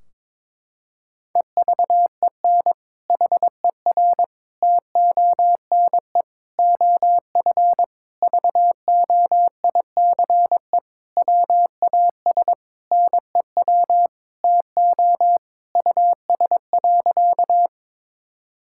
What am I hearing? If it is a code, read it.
Morse code, heard as a keyed tone, about 22 wpm, 700 Hz. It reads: EVEN HER TONE OF VOICE WAS NEW TO US.
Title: Sentences from Top 1600 Words 22wpm